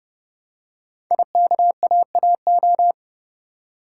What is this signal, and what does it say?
Morse code, heard as a keyed tone, about 30 wpm, 690 Hz. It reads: IKAAO
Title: Code Groups - 5 Letters - 30wpm